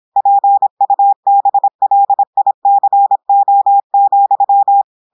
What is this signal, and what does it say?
Morse code, heard as a keyed tone, about 26 wpm, 800 Hz. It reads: PUBLICO,
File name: portugais texte en clair 2 vitesse 26 mots minutes